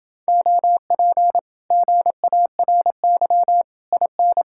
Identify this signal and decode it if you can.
Morse code, heard as a keyed tone, about 29 wpm, 700 Hz. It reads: OP GARY IN